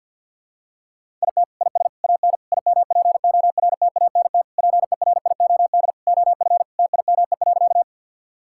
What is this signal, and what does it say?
Morse code, heard as a keyed tone, about 51 wpm, 700 Hz. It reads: IT IS AN IMPORTANT PERIOD OF TIME.
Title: Sentences from Top 1000 Words - Review Part 5 - 50wpm